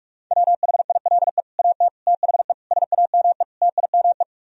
Morse code, heard as a keyed tone, about 44 wpm, 700 Hz. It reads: WHILE AT THE SAME TIME